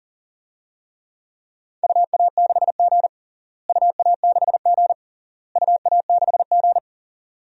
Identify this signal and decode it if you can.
Morse code, heard as a keyed tone, about 40 wpm, 700 Hz. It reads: UA6G UA6G UA6G
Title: DXCC-100 Call Signs - 4 Characters - Repeated 3x - 40wpm